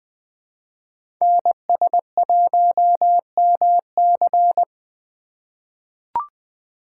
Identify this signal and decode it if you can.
Morse code, heard as a keyed tone, about 20 wpm, 700 Hz. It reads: NS1MC
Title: USA Call Signs - 5 Characters - 3x Character Spacing - 20wpm